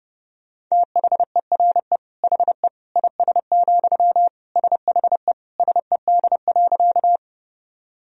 Morse code, heard as a keyed tone, about 32 wpm, 700 Hz. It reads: THERE HE IS, SHE SED.
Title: Sentences from Top 100 Words and Abbreviations 30wpm